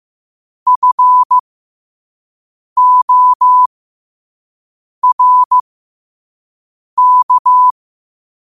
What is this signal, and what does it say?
Morse code, heard as a keyed tone, about 15 wpm, spaced as for 5 wpm, 1000 Hz. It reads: FORK
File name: qotdfiles / 5wpm